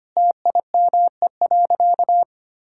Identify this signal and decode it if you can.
Morse code, heard as a keyed tone, about 25 wpm, 700 Hz. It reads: TIME.
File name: practice / 25wpm / Sentences with 3-Word Phrases - 25wpm